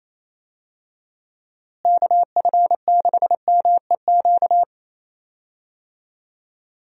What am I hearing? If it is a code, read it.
Morse code, heard as a keyed tone, about 27 wpm, 700 Hz. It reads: KF6MEQ
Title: USA Call Signs - 6 Characters - Repeated 2x - 28wpm